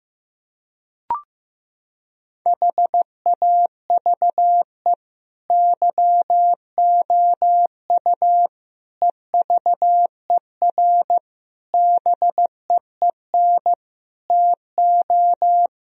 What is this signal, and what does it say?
Morse code, heard as a keyed tone, about 15 wpm, 700 Hz. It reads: HAVE YOU EVER BEEN TO
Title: Sentences from Top 1000 Words - Review Part 1 - 15wpm